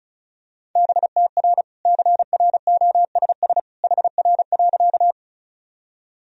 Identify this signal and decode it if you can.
Morse code, heard as a keyed tone, about 36 wpm, 700 Hz. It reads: BTR CROSS HR.
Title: Sentences from Top 400 Words and Abbreviations 35wpm